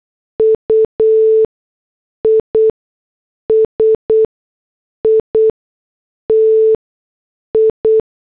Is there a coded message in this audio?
Yes: Morse code from 0.4 to 8.0 s, heard as a keyed tone, about 8 wpm, spaced as for 6 wpm, 440 Hz. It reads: UISITI